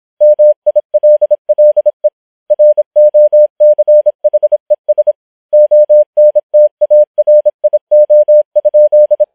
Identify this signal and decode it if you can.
Morse code, heard as a keyed tone, about 26 wpm, 600 Hz. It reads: MILLE ROCHES ONTARIO?